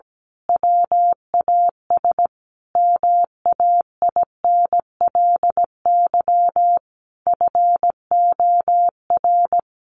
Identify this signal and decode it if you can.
Morse code, heard as a keyed tone, about 17 wpm, 700 Hz. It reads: WAS MAINLY FOR
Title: Sentences from Top 1400 Words 17wpm